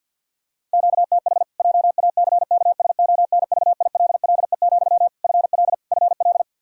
Morse code, heard as a keyed tone, about 51 wpm, 700 Hz. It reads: QTH JACKSONVILLE, FL FL
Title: Practice QSO - Exchange 1 - Beginning - 50wpm